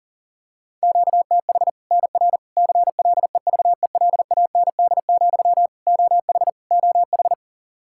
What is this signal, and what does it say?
Morse code, heard as a keyed tone, about 40 wpm, 700 Hz. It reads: QTH NR CLEVELAND, OH OH
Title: Practice QSO - Exchange 1 - Beginning - 40wpm